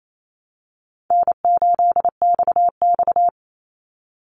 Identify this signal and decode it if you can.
Morse code, heard as a keyed tone, about 28 wpm, 700 Hz. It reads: N8XX
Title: Call Signs Repeated 2x - 28wpm